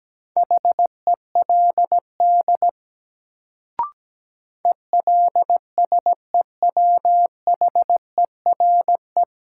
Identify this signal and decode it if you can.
Morse code, heard as a keyed tone, about 17 wpm, 700 Hz. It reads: HELD ELSEWHERE
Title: Sentences from Top 1600 Words - New Word Drill - 17wpm